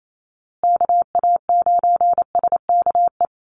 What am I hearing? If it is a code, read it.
Morse code, heard as a keyed tone, about 28 wpm, 700 Hz. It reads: KA9SKE
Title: USA Call Signs Repeated 2x 28wpm